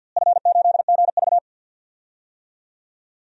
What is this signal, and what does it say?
Morse code, heard as a keyed tone, about 53 wpm, 700 Hz. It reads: F8GV
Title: DXCC-100 Call Signs - 4 Characters - Repeated 2x - 50wpm